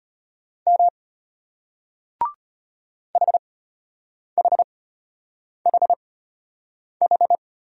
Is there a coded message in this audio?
Yes: Morse code from 0.7 to 7.4 s, heard as a keyed tone, about 33 wpm, spaced as for 8 wpm, 700 Hz. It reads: M HHHH